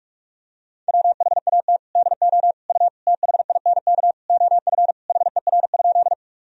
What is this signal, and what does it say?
Morse code, heard as a keyed tone, about 46 wpm, 700 Hz. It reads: WHAT DO U THINK OF HER?
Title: Sentences from Top 200 Words and Abbreviations 45wpm